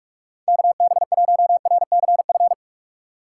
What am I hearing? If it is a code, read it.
Morse code, heard as a keyed tone, about 54 wpm, 700 Hz. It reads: KB1RCF
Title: Speed Racing - USA 2x3 Call Signs 45wpm